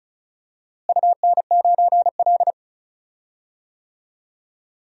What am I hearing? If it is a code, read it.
Morse code, heard as a keyed tone, about 34 wpm, 700 Hz. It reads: UN9L
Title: DXCC-100 Call Signs - 4 Characters - Repeated 3x - 35wpm